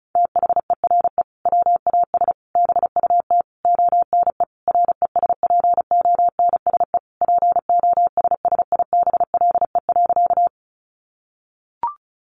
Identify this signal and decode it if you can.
Morse code, heard as a keyed tone, about 34 wpm, 700 Hz. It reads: THERE WAS BUT ONE RESPONSE POSSIBLE.
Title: Sentences from Top 1400 Words 35wpm